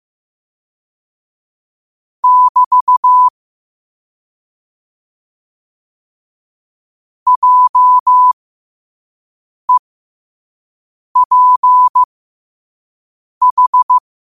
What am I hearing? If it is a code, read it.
Morse code, heard as a keyed tone, about 15 wpm, spaced as for 5 wpm, 1000 Hz. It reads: <BT> JEPH
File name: qotdfiles / 5wpm